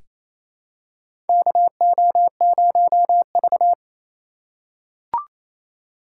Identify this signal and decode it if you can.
Morse code, heard as a keyed tone, about 28 wpm, 700 Hz. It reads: KO0V
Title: USA Call Signs - 4 Characters - 4x Character Spacing - 28wpm